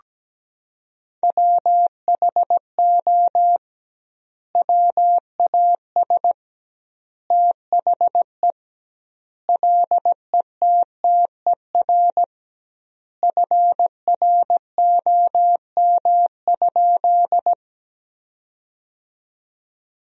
Morse code, heard as a keyed tone, about 17 wpm, 700 Hz. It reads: WHO WAS THE LETTER FROM?
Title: 2x Word-Spacing - Sentences from Top 300 Words - 17wpm